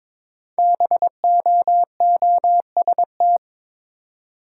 Morse code, heard as a keyed tone, about 22 wpm, 700 Hz. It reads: BOOST
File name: 5 Letter Words - First 3 Letters Given - 22wpm